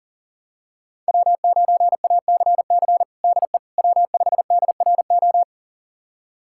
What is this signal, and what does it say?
Morse code, heard as a keyed tone, about 37 wpm, 700 Hz. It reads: W9ACC DE W5DRO
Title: USA Callsign DE Callsign - 5 Characters - 40wpm